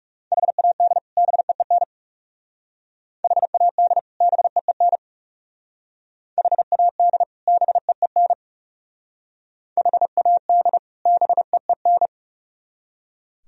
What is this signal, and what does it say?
Morse code, heard as a keyed tone, about 37 wpm, 700 Hz. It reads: HAD BEEN HAD BEEN HAD BEEN HAD BEEN
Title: Speed-Racing - Sets of 2 Words 30wpm